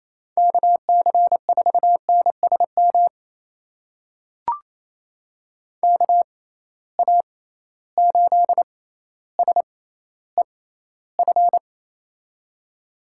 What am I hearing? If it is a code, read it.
Morse code, heard as a keyed tone, about 28 wpm, spaced as for 14 wpm, 700 Hz. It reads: KC4NSM KA8SEF
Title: USA Call Signs - 6 Characters - 6x Character Spacing - 28wpm